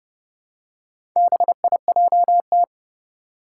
Morse code, heard as a keyed tone, about 30 wpm, 700 Hz. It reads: BIJT
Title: Four Challenging Letter-Number Combinations 30wpm